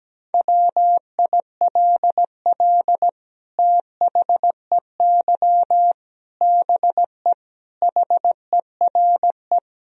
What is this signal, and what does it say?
Morse code, heard as a keyed tone, about 17 wpm, 700 Hz. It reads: WILL THEY BE HERE
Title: Sentences from Top 1000 Words - Review Part 4 - 17wpm